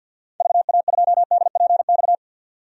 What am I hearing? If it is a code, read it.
Morse code, heard as a keyed tone, about 46 wpm, 700 Hz. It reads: VA2DPX
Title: DXCC-100 Call Signs - 6 Characters - 50wpm